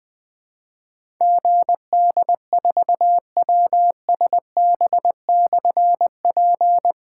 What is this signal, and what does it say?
Morse code, heard as a keyed tone, about 20 wpm, 700 Hz. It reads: GD4WSB/P
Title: Difficult Call Signs 20wpm